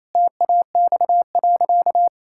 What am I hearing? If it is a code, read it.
Morse code, heard as a keyed tone, about 28 wpm, 700 Hz. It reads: TAX.
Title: Sentences from Top 1000 Words - Review Part 5 - 28wpm